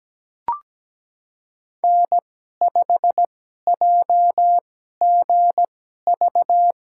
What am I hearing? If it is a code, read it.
Morse code, heard as a keyed tone, about 17 wpm, spaced as for 12 wpm, 700 Hz. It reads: N5JGV
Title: USA Call Signs - 5 Characters - 2x Character Spacing - 17wpm